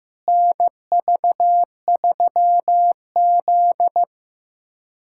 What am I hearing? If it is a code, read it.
Morse code, heard as a keyed tone, about 15 wpm, 700 Hz. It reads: NV3Z
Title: USA Callsign DE Callsign - 4 Characters - 15wpm